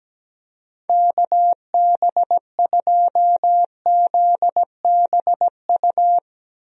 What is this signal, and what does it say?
Morse code, heard as a keyed tone, about 17 wpm, 700 Hz. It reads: KB2ZBU
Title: USA Call Signs - 6 Characters - Repeated 3x - 17wpm